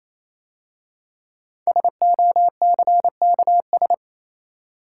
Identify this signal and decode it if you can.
Morse code, heard as a keyed tone, about 27 wpm, 690 Hz. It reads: SOCKS
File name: Easy Words - Speed-Racing with Hints - 22wpm